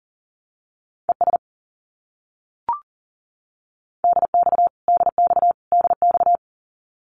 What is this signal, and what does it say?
Morse code, heard as a keyed tone, about 40 wpm, 700 Hz. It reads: ES DX DX DX
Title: Abbreviations Repeated 3x - 40wpm